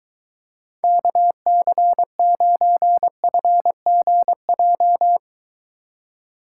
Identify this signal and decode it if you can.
Morse code, heard as a keyed tone, about 23 wpm, 700 Hz. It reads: KC9FGJ